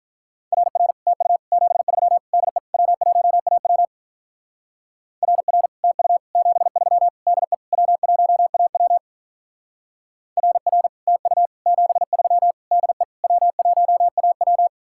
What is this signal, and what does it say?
Morse code, heard as a keyed tone, about 43 wpm, 700 Hz. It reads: RR TU 73 DE W1AW RR TU 73 DE W1AW RR TU 73 DE W1AW
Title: Speed-Racing Common QSO Phrases Encore - 35wpm